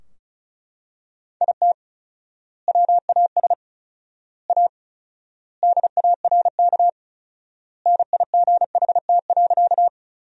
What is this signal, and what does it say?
Morse code, heard as a keyed tone, about 36 wpm, 700 Hz. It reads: IT WAS A DARK NIGHT.
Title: 4x Word-Spacing - Sentences from Top 500 Words - 35wpm